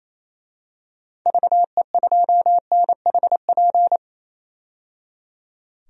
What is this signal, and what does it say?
Morse code, heard as a keyed tone, about 28 wpm, 700 Hz. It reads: VE2NHP
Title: DXCC-100 Call Signs - 6 Characters - 28wpm